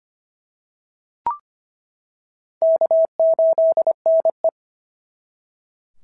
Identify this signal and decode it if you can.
Morse code, heard as a keyed tone, about 24 wpm, 650 Hz. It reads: K8NE
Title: SOTA QSO Practice 25wpm